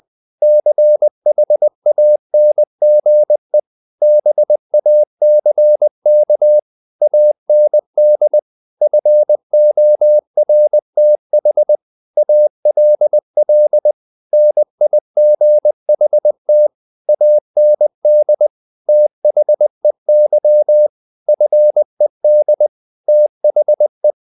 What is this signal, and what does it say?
Morse code, heard as a keyed tone, about 20 wpm, 600 Hz. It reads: CHANGE BACK AND FORTH ALL NIGHT AND THEY FED THE